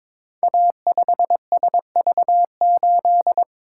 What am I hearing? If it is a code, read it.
Morse code, heard as a keyed tone, about 22 wpm, 700 Hz. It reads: A5SV8
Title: Code Groups - 5 Letter:Numbers - 22wpm